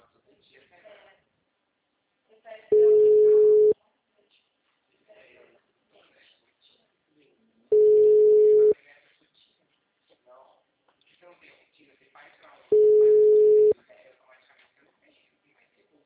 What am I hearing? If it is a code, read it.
Ringback tone (Europe, 425 Hz), with speech.
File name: out- -2017- - - .21497